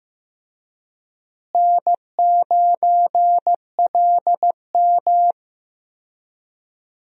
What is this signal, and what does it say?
Morse code, heard as a keyed tone, about 15 wpm, 700 Hz. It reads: N9LM
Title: USA Call Signs - 4 Characters - Repeated 3x - 15wpm